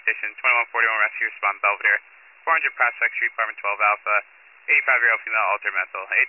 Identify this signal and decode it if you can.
Emergency Alert System (SAME) tones (header bursts), heard as harsh buzzing bursts, with speech. It is heard from 0.7 to 5.2 s.